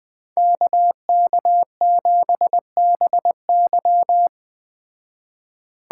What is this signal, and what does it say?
Morse code, heard as a keyed tone, about 20 wpm, 700 Hz. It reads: KK7BY
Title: Speed Racing - USA 2x2 Call Signs 17wpm